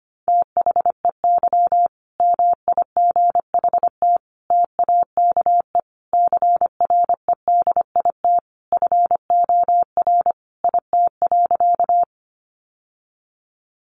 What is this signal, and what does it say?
Morse code, heard as a keyed tone, about 25 wpm, 700 Hz. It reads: THEY MIGHT TAKE CREDIT FOR IT.
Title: Sentences from Top 1000 Words 25wpm